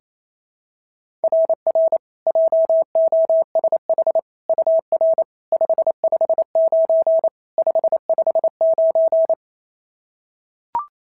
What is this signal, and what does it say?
Morse code, heard as a keyed tone, about 28 wpm, 690 Hz. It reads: RR JOSH UR 559 559
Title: SOTA QSO Practice 28wpm